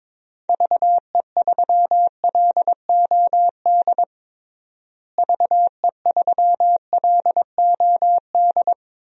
Morse code, heard as a keyed tone, about 22 wpm, 700 Hz. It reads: VE3LOD VE3LOD
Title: DXCC-100 Call Signs - 6 Characters - Repeated 2x - 22wpm